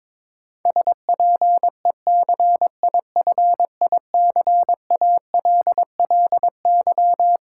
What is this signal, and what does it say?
Morse code, heard as a keyed tone, about 22 wpm, 700 Hz. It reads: SPECIFICALLY
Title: Sentences from Top 1100 Words - New Word Drill - 22wpm